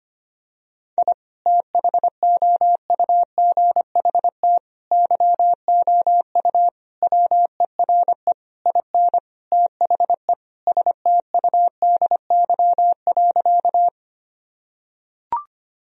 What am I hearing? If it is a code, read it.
Morse code, heard as a keyed tone, about 25 wpm, 700 Hz. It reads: I THOUGHT YOU WERE IN THE STUDY.
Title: Sentences from Top 1000 Words - Review Part 1 - 25wpm